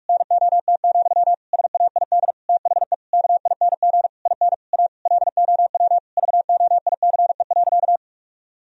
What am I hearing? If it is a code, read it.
Morse code, heard as a keyed tone, about 41 wpm, 700 Hz. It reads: NOT, SAID THE KING IN A LOW VOICE.
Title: Sentences from Top 1000 Words - Review Part 1 - 45wpm